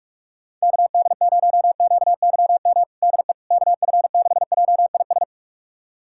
Morse code, heard as a keyed tone, about 46 wpm, 700 Hz. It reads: KD0QYM DE KF6JIS
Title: USA Callsign DE Callsign - 6 Characters - 45wpm